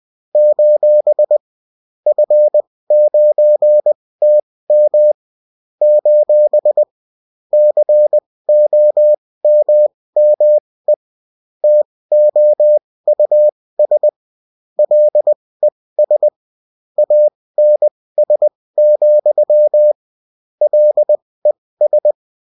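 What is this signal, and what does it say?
Morse code, heard as a keyed tone, about 20 wpm, spaced as for 16 wpm, 600 Hz. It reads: : F9TM : COMME TOUS LES ANS, LES
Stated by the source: UFT bulletin sent in CW.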